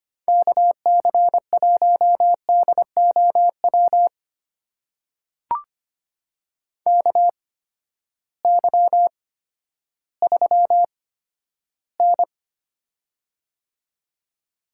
Morse code, heard as a keyed tone, about 25 wpm, spaced as for 12 wpm, 700 Hz. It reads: KC1DOW KY3N
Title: USA Call Signs - 8x Character Spacing - 25wpm